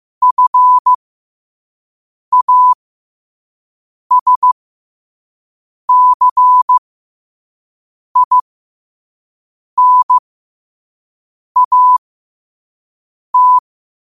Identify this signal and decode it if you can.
Morse code, heard as a keyed tone, about 15 wpm, spaced as for 5 wpm, 1000 Hz. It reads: FASCINAT
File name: qotdfiles / 5wpm